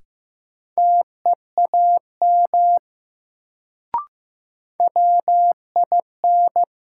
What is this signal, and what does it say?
Morse code, heard as a keyed tone, about 15 wpm, 700 Hz. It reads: TEAM WIN
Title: Sentences from Top 1500 Words - New Word Drill Alt - 15wpm